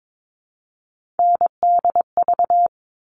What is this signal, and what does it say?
Morse code, heard as a keyed tone, about 22 wpm, 700 Hz. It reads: NDV
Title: All Cut Numbers - 3-Digit 22wpm